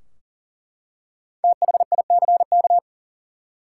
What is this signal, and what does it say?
Morse code, heard as a keyed tone, about 40 wpm, 700 Hz. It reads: THICK
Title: Five Letter Words - 40wpm